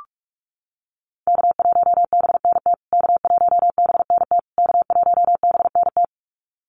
Morse code, heard as a keyed tone, about 44 wpm, 700 Hz. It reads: K1BNT K1BNT K1BNT
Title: USA Call Signs Repeated 3x 45wpm